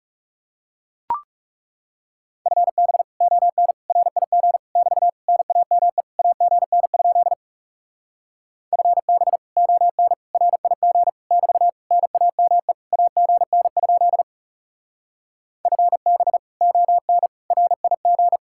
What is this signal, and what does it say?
Morse code, heard as a keyed tone, about 39 wpm, 700 Hz. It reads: FB ON RIG <BT> NAME AGN? FB ON RIG <BT> NAME AGN? FB ON RIG